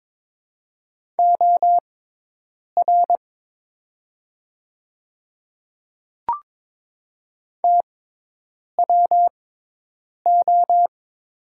Morse code, heard as a keyed tone, about 23 wpm, spaced as for 8 wpm, 700 Hz. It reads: OR TWO
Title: 6x Character Spacing - Top 100 Words - 22wpm